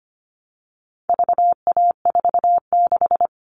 Morse code, heard as a keyed tone, about 25 wpm, 700 Hz. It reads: VA46
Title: All Cut Numbers - 4-Digit 25wpm